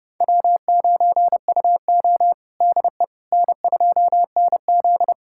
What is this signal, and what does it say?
Morse code, heard as a keyed tone, about 31 wpm, 700 Hz. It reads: W9UO DE N2NZ
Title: USA Callsign DE Callsign - 4 Characters - 30wpm